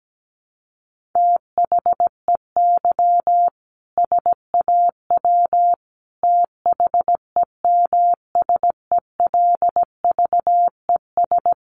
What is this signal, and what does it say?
Morse code, heard as a keyed tone, about 17 wpm, 700 Hz. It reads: THEY SAW THEMSELVES
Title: A Sound Of Thunder 17wpm